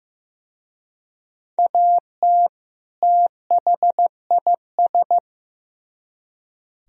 Morse code, heard as a keyed tone, about 18 wpm, 700 Hz. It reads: AT THIS
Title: Speed-Racing - Sets of 2 Words 15wpm